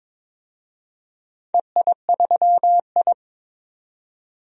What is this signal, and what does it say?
Morse code, heard as a keyed tone, about 22 wpm, 700 Hz. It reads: EI3I
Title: DXCC-100 Call Signs - 4 Characters - 22wpm